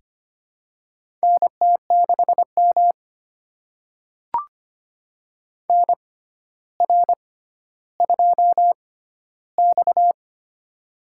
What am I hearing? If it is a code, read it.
Morse code, heard as a keyed tone, about 25 wpm, spaced as for 13 wpm, 700 Hz. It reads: NT6M NR2X
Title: USA Call Signs - 4 Characters - 6x Character Spacing - 25wpm